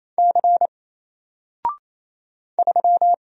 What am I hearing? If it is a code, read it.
Morse code, heard as a keyed tone, about 27 wpm, 700 Hz. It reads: C 3
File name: Single-Letter-Number-28wpm